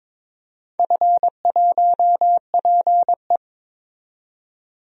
Morse code, heard as a keyed tone, about 21 wpm, 700 Hz. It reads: F1PE
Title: DXCC-100 Call Signs - 4 Characters - 22wpm